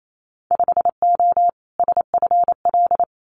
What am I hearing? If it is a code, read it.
Morse code, heard as a keyed tone, about 28 wpm, 700 Hz. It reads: 5O SFL
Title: Winter Field Day Exchanges 28wpm